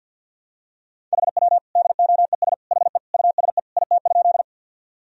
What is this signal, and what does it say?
Morse code, heard as a keyed tone, about 50 wpm, 700 Hz. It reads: HW DOES HE USE IT?
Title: Sentences from Top 200 Words and Abbreviations 50wpm